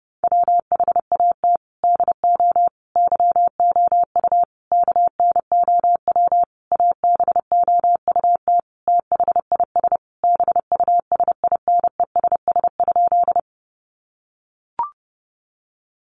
Morse code, heard as a keyed tone, about 28 wpm, 700 Hz. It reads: WHAT DO YOU KNOW ABOUT THIS BUSINESS?
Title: Sentences from Top 600 Words 30wpm